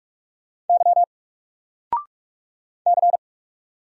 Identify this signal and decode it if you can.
Morse code, heard as a keyed tone, about 44 wpm, 700 Hz. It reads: Y C
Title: Single Character (Letter:Number:Punctuation:Prosign) 45wpm